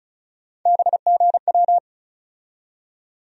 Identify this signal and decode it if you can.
Morse code, heard as a keyed tone, about 36 wpm, 700 Hz. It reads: BGW
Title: USA Call Sign Postfixes - 3 Letters 35wpm